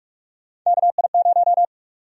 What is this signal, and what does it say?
Morse code, heard as a keyed tone, about 44 wpm, 700 Hz. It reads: KI0
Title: USA Call Sign Prefixes - 45wpm